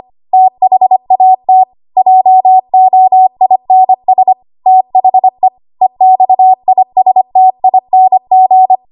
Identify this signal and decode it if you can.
Morse code, heard as a keyed tone, about 25 wpm, 750 Hz. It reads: THAT JOINS THE EXISTING
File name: Archive / 25 WPM / 25WPM